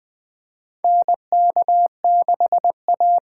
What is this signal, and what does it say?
Morse code, heard as a keyed tone, about 20 wpm, 700 Hz. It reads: NK6A
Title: 2023 SOTA Chaser Call Signs - 20wpm